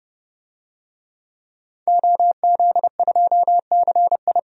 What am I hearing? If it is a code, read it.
Morse code, heard as a keyed tone, about 32 wpm, 700 Hz. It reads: OZ2CI
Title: DXCC-100 Call Signs - 5 Characters - 30wpm